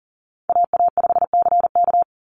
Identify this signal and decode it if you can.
Morse code, heard as a keyed tone, about 40 wpm, 700 Hz. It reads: AA5CK
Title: NA SOTA Chaser Callsigns 40wpm